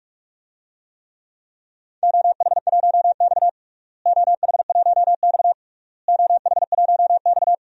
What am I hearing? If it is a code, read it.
Morse code, heard as a keyed tone, about 41 wpm, 700 Hz. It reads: OH1X OH1X OH1X
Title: DXCC-100 Call Signs - 4 Characters - Repeated 3x - 45wpm